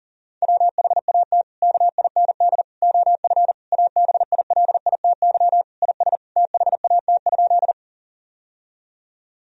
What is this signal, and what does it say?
Morse code, heard as a keyed tone, about 40 wpm, 700 Hz. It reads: WHAT KIND OF ABILITY IS THAT?
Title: Sentences from Top 1000 Words - Review Part 3 - 40wpm